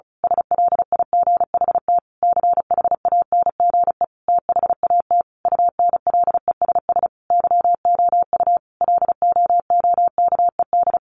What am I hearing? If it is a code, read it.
Morse code, heard as a keyed tone, about 34 wpm, 700 Hz. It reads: SLIGHT CHANGE THAT UNLESS YOU LOOKED
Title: A Sound Of Thunder 35wpm